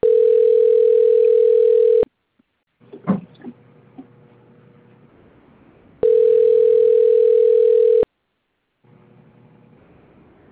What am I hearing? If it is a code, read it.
Ringback tone (North America), 440 + 480 Hz.